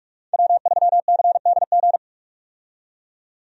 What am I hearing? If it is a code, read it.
Morse code, heard as a keyed tone, about 46 wpm, 700 Hz. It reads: W3CDG
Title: USA Call Signs - 5 Characters - 45wpm